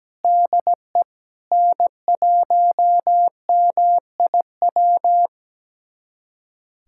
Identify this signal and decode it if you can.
Morse code, heard as a keyed tone, about 17 wpm, 700 Hz. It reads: DE N1MIW
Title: USA Callsign DE Callsign - 5 Characters - 17wpm